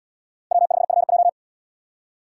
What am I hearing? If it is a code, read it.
Morse code, heard as a keyed tone, about 69 wpm, 700 Hz. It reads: W5FJ